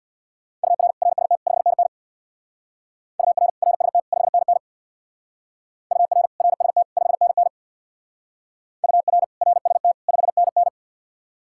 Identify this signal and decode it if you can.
Morse code, heard as a keyed tone, about 66 wpm, 700 Hz. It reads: UR RST 5NN UR RST 5NN UR RST 5NN UR RST 5NN